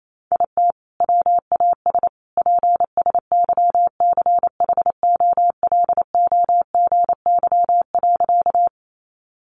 Morse code, heard as a keyed tone, about 28 wpm, 700 Hz. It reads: IT WAS PSYCHOLOGY.
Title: Sentences from Top 1400 Words 28wpm